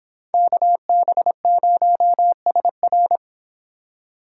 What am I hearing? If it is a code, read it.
Morse code, heard as a keyed tone, about 24 wpm, 700 Hz. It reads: KB0SR